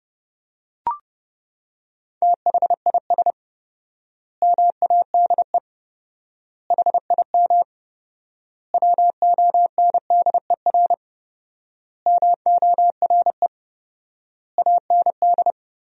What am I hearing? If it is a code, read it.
Morse code, heard as a keyed tone, about 32 wpm, 700 Hz. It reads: THIS MADE HIM WONDER MORE AND
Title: 4x Word-Spacing - Sentences from Top 500 Words - 30wpm